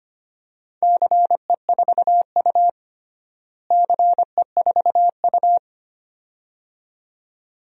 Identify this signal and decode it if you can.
Morse code, heard as a keyed tone, about 25 wpm, 700 Hz. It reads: CE4U CE4U
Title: DXCC-100 Call Signs - 4 Characters - Repeated 2x - 25wpm